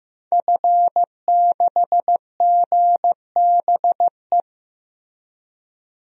Morse code, heard as a keyed tone, about 15 wpm, 700 Hz. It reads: F6GBE
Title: DXCC-100 Call Signs - 5 Characters - 15wpm